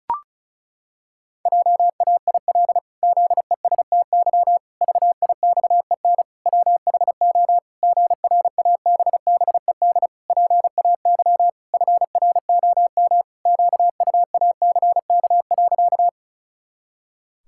Morse code, heard as a keyed tone, about 34 wpm, 700 Hz. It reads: JAIL ZESTY VIXEN WHO GRABBED PAY FROM QUACK.
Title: Pangram Challenge v2 - 35wpm